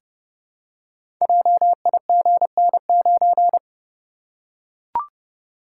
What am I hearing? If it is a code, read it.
Morse code, heard as a keyed tone, about 30 wpm, 700 Hz. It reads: JIGN9
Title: Code Groups - 5 Letter:Numbers - 30wpm